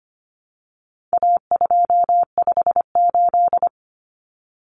Morse code, heard as a keyed tone, about 25 wpm, 700 Hz. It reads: A258
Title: Common Cut Numbers - 4-Digit 25wpm